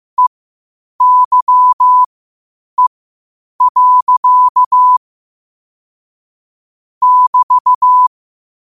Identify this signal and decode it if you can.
Morse code, heard as a keyed tone, about 15 wpm, spaced as for 8 wpm, 1000 Hz. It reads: EYE. <BT>
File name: qotdfiles / 8wpm